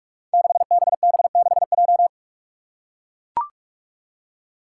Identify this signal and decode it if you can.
Morse code, heard as a keyed tone, about 44 wpm, 700 Hz. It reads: 6BB6J
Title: Five Challenging Letter-Number Combinations 45wpm